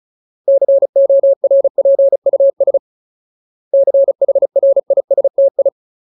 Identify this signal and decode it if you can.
Morse code, heard as a keyed tone, about 34 wpm, 550 Hz. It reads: CORPUS CHRISTI
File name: Ultimate Word List / 35WPM Ultimate